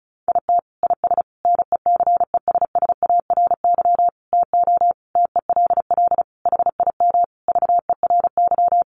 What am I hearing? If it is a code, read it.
Morse code, heard as a keyed tone, about 34 wpm, 700 Hz. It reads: IT IS NECESSARY TO TELL HIM VERY
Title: Sentences from Top 800 Words 35wpm